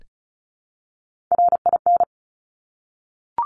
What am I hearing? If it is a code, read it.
Morse code, heard as a keyed tone, about 36 wpm, 700 Hz. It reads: RIN
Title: Top Three Letter Combinations in Words 35wpm